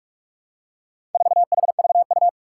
Morse code, heard as a keyed tone, about 44 wpm, 700 Hz. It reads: 4HVU
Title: Four Challenging Letter-Number Combinations 45wpm